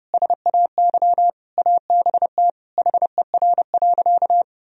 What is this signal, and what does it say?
Morse code, heard as a keyed tone, about 32 wpm, 700 Hz. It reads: SAY ABT HER.
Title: Sentences from Top 200 Words and Abbreviations 30wpm